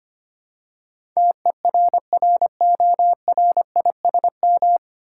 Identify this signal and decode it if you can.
Morse code, heard as a keyed tone, about 25 wpm, 700 Hz. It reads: TERRORISM
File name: Nine Letter Words - 25wpm